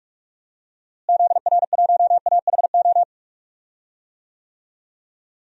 Morse code, heard as a keyed tone, about 46 wpm, 700 Hz. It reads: ZR1AHO
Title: DXCC-100 Call Signs - 6 Characters - Repeated 2x - 45wpm